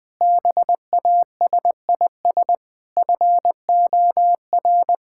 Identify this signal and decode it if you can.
Morse code, heard as a keyed tone, about 21 wpm, 700 Hz. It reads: BASIS FOR
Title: Sentences from Top 1000 Words - Review Part 5 - 20wpm